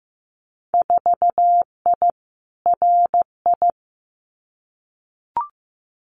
Morse code, heard as a keyed tone, about 15 wpm, 700 Hz. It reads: 4I RI
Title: Winter Field Day Exchanges Encore 15wpm